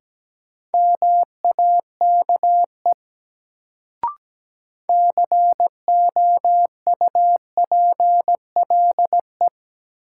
Morse code, heard as a keyed tone, about 17 wpm, 700 Hz. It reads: MAKE COUPLE
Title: Sets of 2 Words - New Word Drill - 17wpm